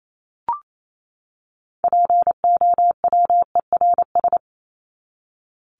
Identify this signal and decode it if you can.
Morse code, heard as a keyed tone, about 28 wpm, 700 Hz. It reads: POWERS
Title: ES:S Suffix - Top 100 Words with Base Words 28wpm